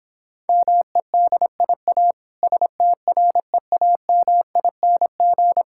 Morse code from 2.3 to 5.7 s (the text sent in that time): STREAMING